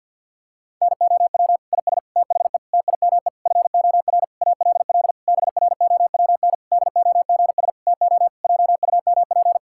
Morse code, heard as a keyed tone, about 50 wpm, 700 Hz. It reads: NOW IS THE TIME FOR ALL BROWN DOGS TO JUMP
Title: Pangram Challenge v2 - 50wpm